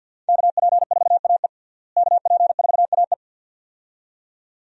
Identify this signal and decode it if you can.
Morse code, heard as a keyed tone, about 52 wpm, 700 Hz. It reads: KP4AE KP4AE
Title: USA Call Signs - 5 Characters - Repeated 2x - 50wpm